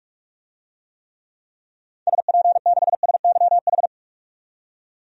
Morse code, heard as a keyed tone, about 44 wpm, 700 Hz. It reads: SP6SYH
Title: DXCC-100 Call Signs - 6 Characters - 45wpm